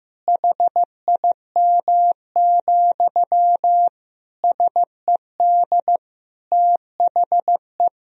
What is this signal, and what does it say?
Morse code, heard as a keyed tone, about 15 wpm, 700 Hz. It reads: HIM, SED THE
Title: Sentences from Top 400 Words and Abbreviations 15wpm